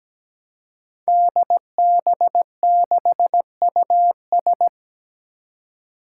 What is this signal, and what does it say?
Morse code, heard as a keyed tone, about 17 wpm, 700 Hz. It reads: DB6US
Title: Five Challenging Letter-Number Combinations 17wpm